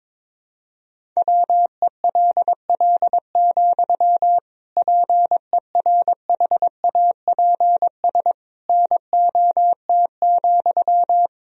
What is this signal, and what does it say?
Morse code, heard as a keyed tone, about 22 wpm, 700 Hz. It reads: WELL, PERHAPS NOT,
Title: Sentences from Top 1000 Words - Review Part 2 - 22wpm